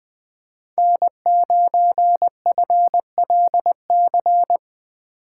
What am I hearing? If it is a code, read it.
Morse code, heard as a keyed tone, about 20 wpm, 700 Hz. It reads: N9FLC
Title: USA Call Signs - 2x Character Spacing - 20wpm